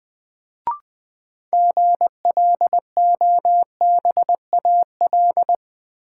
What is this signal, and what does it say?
Morse code, heard as a keyed tone, about 21 wpm, 700 Hz. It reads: GLOBAL
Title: Sentences from Top 1200 Words - New Word Drill - 20wpm